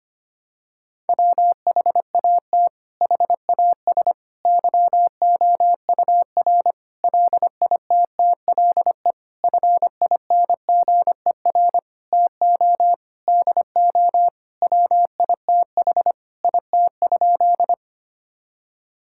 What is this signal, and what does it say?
Morse code, heard as a keyed tone, about 25 wpm, 700 Hz. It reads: WHAT HAS YOUR LITTLE FINGER TO DO WITH IT?
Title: Sentences from Top 1600 Words 25wpm